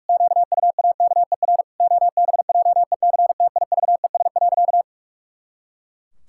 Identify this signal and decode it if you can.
Morse code, heard as a keyed tone, about 46 wpm, 700 Hz. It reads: QUAKER OBJECTIVES.
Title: Pangram Challenge v2 - 45wpm